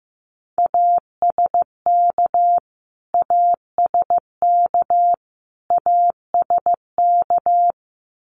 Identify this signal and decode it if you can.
Morse code, heard as a keyed tone, about 15 wpm, 700 Hz. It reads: ASK ASK ASK
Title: Top 300 Words Repeated 15wpm